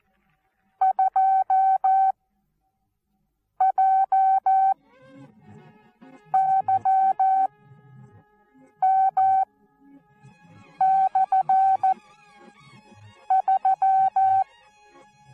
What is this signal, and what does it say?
Morse code, heard as a keyed tone, about 14 wpm, spaced as for 5 wpm, 770 Hz. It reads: 2JYM/3
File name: Onlinelehrgang / morsen / DJ4UF Morsen19